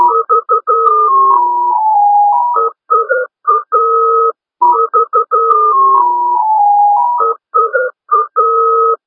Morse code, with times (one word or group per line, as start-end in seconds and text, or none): none